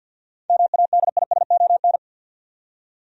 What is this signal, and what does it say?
Morse code, heard as a keyed tone, about 49 wpm, 700 Hz. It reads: MADISON
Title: USA Common Names - 50wpm